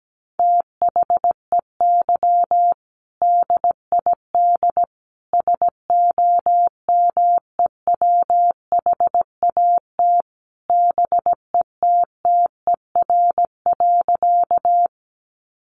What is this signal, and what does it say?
Morse code, heard as a keyed tone, about 17 wpm, 700 Hz. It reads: THEY DID SOMEWHAT BETTER.
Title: Sentences from Top 1300 Words 17wpm